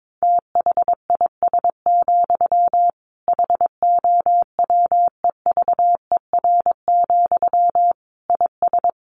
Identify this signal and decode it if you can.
Morse code, heard as a keyed tone, about 22 wpm, 700 Hz. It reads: THIS, HOWEVER, IS
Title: Sentences from Top 1500 Words 22wpm